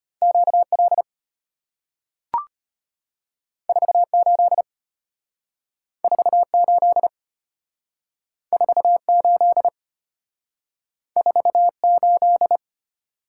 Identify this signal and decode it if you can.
Morse code, heard as a keyed tone, about 33 wpm, 690 Hz. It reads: QL 48 48 48 48